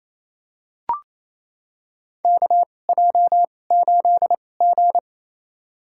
Morse code, heard as a keyed tone, about 28 wpm, spaced as for 20 wpm, 690 Hz. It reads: KJ8G
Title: USA Call Signs - 4 Characters - 2x Character Spacing - 28wpm